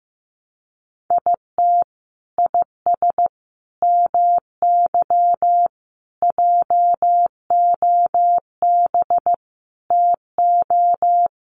Morse code, heard as a keyed tone, about 15 wpm, 700 Hz. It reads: IT IS MY JOB TO
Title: Sentences from Top 600 Words 15wpm